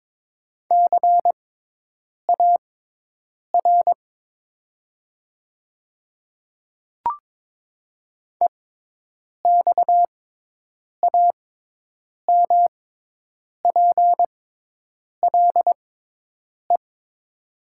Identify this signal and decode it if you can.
Morse code, heard as a keyed tone, about 22 wpm, spaced as for 8 wpm, 700 Hz. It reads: CAR EXAMPLE
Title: 6x Character Spacing - Top 300 Words - 22wpm